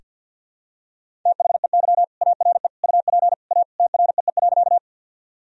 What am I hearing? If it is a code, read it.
Morse code, heard as a keyed tone, about 47 wpm, 700 Hz. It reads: THEY ARE UP A TREE.
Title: Sentences from Top 1000 Words - Review Part 1 - 50wpm